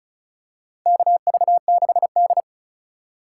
Morse code, heard as a keyed tone, about 34 wpm, 700 Hz. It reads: KV6D
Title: USA Call Signs - 4 Characters - 3x Character Spacing - 35wpm